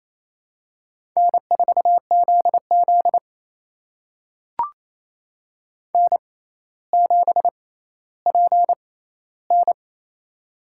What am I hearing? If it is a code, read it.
Morse code, heard as a keyed tone, about 28 wpm, spaced as for 14 wpm, 700 Hz. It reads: N4ZZ N7PN
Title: USA Call Signs - 4 Characters - 6x Character Spacing - 28wpm